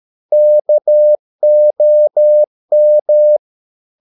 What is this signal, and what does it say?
Morse code, heard as a keyed tone, about 13 wpm, 600 Hz. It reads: KOM
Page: Ferie 13 wpm